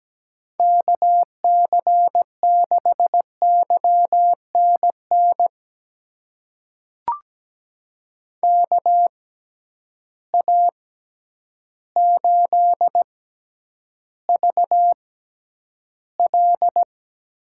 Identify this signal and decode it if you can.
Morse code, heard as a keyed tone, about 17 wpm, spaced as for 9 wpm, 700 Hz. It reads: KC6YNN KA8VL
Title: USA Call Signs - 6 Characters - 6x Character Spacing - 17wpm